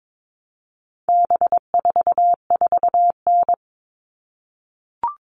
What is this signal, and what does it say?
Morse code, heard as a keyed tone, about 22 wpm, 700 Hz. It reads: B44N
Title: All Cut Numbers - 4-Digit 22wpm